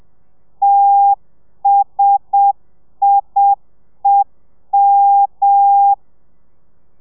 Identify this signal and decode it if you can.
Morse code, heard as a keyed tone, about 7 wpm, 800 Hz. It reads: TSIEM
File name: telegrafia / i2tfj / Lez1 30